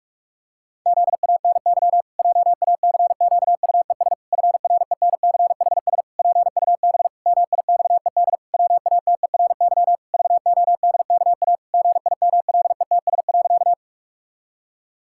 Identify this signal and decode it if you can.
Morse code, heard as a keyed tone, about 46 wpm, 700 Hz. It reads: ZANY JACQUES FRENCHS PUB MIXED WATERY VODKA GIMLETS.
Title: Pangram Challenge v2 - 45wpm